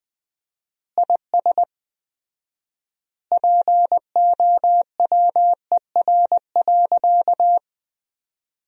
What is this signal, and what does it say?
Morse code, heard as a keyed tone, about 20 wpm, 700 Hz. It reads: IS POWER.
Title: 4x Word-Spacing - Sentences from Top 500 Words - 20wpm